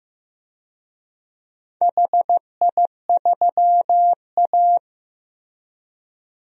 Morse code, heard as a keyed tone, about 16 wpm, 700 Hz. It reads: HI3A
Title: DXCC-100 Call Signs - 4 Characters - Repeated 2x - 15wpm